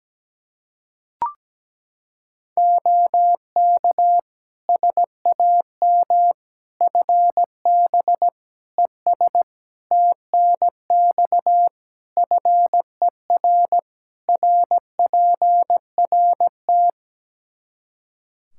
Morse code, heard as a keyed tone, about 17 wpm, 700 Hz. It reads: OK SAM FB ES TNX FER RPRT
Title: Practice QSO - Exchange 2 - Info - 17wpm